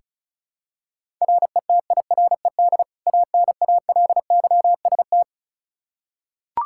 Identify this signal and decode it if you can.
Morse code, heard as a keyed tone, about 34 wpm, 700 Hz. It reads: RETIRED ANALYST
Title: Practice QSO - Exchange 3 - More Info - 35wpm